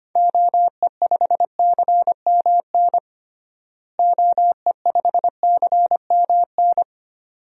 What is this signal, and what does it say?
Morse code, heard as a keyed tone, about 25 wpm, 700 Hz. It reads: OE5CMN OE5CMN
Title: DXCC-100 Call Signs - 6 Characters - Repeated 2x - 25wpm